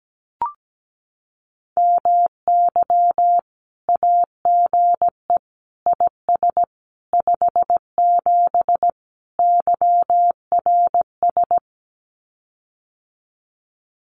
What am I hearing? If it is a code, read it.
Morse code, heard as a keyed tone, about 17 wpm, 700 Hz. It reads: MY AGE IS 57 YRS
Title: Common QSO Phrases Encore 17wpm